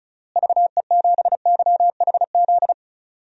Morse code, heard as a keyed tone, about 36 wpm, 700 Hz. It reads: VE7YHZ
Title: DXCC-100 Call Signs - 6 Characters - Repeated 3x - 35wpm